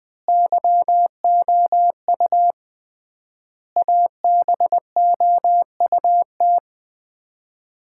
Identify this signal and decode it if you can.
Morse code, heard as a keyed tone, about 20 wpm, 700 Hz. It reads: YOU ABOUT
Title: 3x Word-Spacing - Sets of 5 Words - 20wpm